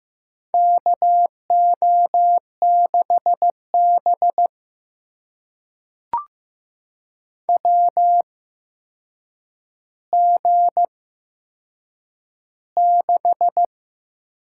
Morse code, heard as a keyed tone, about 15 wpm, spaced as for 7 wpm, 700 Hz. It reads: KO6B WG6
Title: USA Call Signs - 4 Characters - 8x Character Spacing - 15wpm